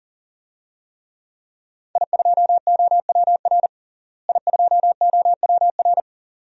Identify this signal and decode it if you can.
Morse code, heard as a keyed tone, about 40 wpm, 700 Hz. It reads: I2OWR I2OWR
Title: DXCC-100 Call Signs - 5 Characters - Repeated 2x - 40wpm